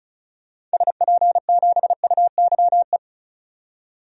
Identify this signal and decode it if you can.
Morse code, heard as a keyed tone, about 34 wpm, 700 Hz. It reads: SP7UYE
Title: DXCC-100 Call Signs - 6 Characters - 35wpm